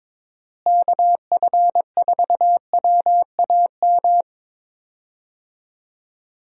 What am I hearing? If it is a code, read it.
Morse code, heard as a keyed tone, about 22 wpm, 700 Hz. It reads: KF4WAM
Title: USA Call Signs - 6 Characters - Repeated 2x - 22wpm